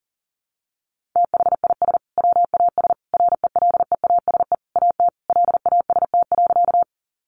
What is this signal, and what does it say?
Morse code, heard as a keyed tone, about 40 wpm, 700 Hz. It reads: THIS WAS RELEASE AT LAST.
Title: Sentences from Top 1100 Words 40wpm